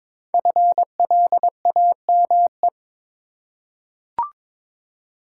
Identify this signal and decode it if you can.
Morse code, heard as a keyed tone, about 22 wpm, 700 Hz. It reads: FLAME
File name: 5 Letter Words - First 3 Letters Given - 22wpm